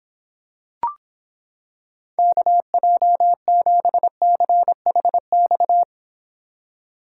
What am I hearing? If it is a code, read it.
Morse code, heard as a keyed tone, about 26 wpm, 700 Hz. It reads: KJ7CHX